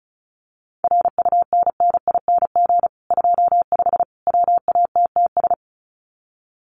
Morse code, heard as a keyed tone, about 34 wpm, 700 Hz. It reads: RUNNING 25 WATTS
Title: Common QSO Phrases 35wpm